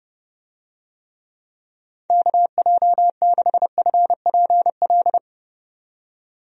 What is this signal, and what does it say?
Morse code, heard as a keyed tone, about 32 wpm, 700 Hz. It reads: KJ6FPL
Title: USA Call Signs - 6 Characters - 30wpm